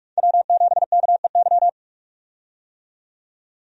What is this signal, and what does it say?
Morse code, heard as a keyed tone, about 46 wpm, 700 Hz. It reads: W7KEY
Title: USA Call Signs - 5 Characters - 45wpm